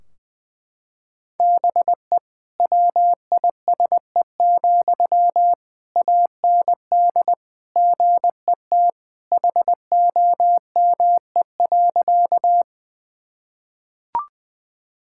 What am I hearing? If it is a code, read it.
Morse code, heard as a keyed tone, about 20 wpm, 700 Hz. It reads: BE WISE, AND GET HOME.
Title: Sentences from Top 1600 Words 20wpm